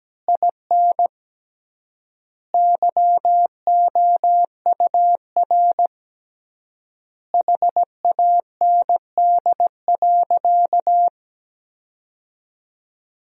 Morse code, heard as a keyed tone, about 17 wpm, 700 Hz. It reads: IN YOUR HAND.
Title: 3x Word-Spacing - Sentences from Top 400 Words - 17wpm